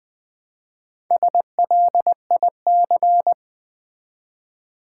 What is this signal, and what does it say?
Morse code, heard as a keyed tone, about 20 wpm, 700 Hz. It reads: SLIC
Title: Build A Word - 1 Letter At A Time - 5-Letter Words - 20wpm